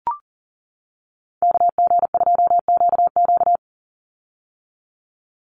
Morse code, heard as a keyed tone, about 40 wpm, 690 Hz. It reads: KG2QQ
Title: USA Call Signs 40wpm